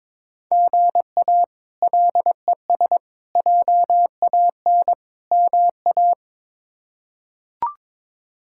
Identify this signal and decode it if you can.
Morse code, heard as a keyed tone, about 22 wpm, 700 Hz. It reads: GA LES JAN MA
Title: K1USN SST QSO Practice - 22wpm